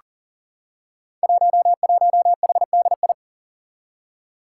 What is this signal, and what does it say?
Morse code, heard as a keyed tone, about 40 wpm, 700 Hz. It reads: 11HDI
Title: Five Challenging Letter-Number Combinations 40wpm